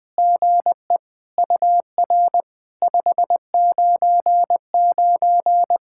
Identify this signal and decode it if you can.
Morse code, heard as a keyed tone, about 20 wpm, 700 Hz. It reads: GE UR 599
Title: POTA QSO Practice - 20wpm